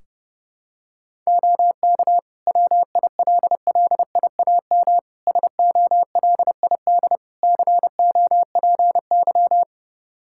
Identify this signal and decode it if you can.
Morse code, heard as a keyed tone, about 30 wpm, 700 Hz. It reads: OK WILLIAM SOLID COPY
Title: Practice QSO - Exchange 3 - More Info - 30wpm